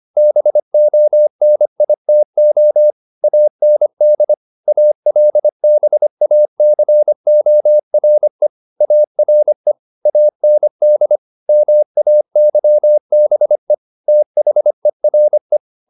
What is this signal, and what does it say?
Morse code, heard as a keyed tone, about 25 wpm, 600 Hz. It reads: BONITO AND ALBACORE ARE AND MAYBE THERE
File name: the old man and the sea part 1 25wpm